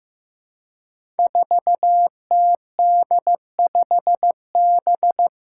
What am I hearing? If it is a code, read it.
Morse code, heard as a keyed tone, about 15 wpm, 700 Hz. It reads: 4TD5B
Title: Five Challenging Letter-Number Combinations 15wpm